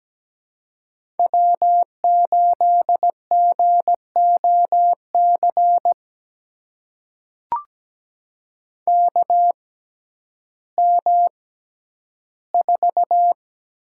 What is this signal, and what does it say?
Morse code, heard as a keyed tone, about 17 wpm, spaced as for 10 wpm, 700 Hz. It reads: W8GOC KM4
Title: USA Call Signs - 5 Characters - 6x Character Spacing - 17wpm